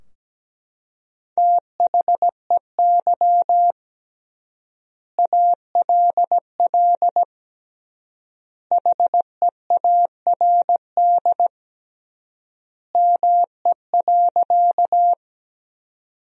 Morse code, heard as a keyed tone, about 17 wpm, 700 Hz. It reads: THEY ALL HEARD ME.
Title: 3x Word-Spacing - Sentences from Top 400 Words - 17wpm